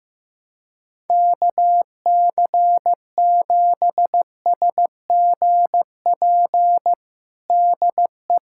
Morse code, heard as a keyed tone, about 15 wpm, 700 Hz. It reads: KC7SGP DE
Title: USA Callsign DE Callsign 15wpm